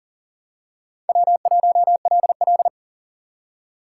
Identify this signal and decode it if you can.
Morse code, heard as a keyed tone, about 40 wpm, 700 Hz. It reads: W1LL
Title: USA Call Signs - 4 Characters - 2x Character Spacing - 40wpm